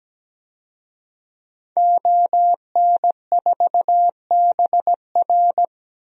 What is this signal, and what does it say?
Morse code, heard as a keyed tone, about 17 wpm, 700 Hz. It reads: ON4BR
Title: DXCC-100 Call Signs - 5 Characters - 17wpm